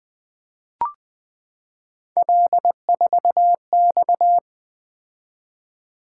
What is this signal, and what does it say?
Morse code, heard as a keyed tone, about 20 wpm, 700 Hz. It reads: L4X
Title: Three Letter-Number Combinations - 20wpm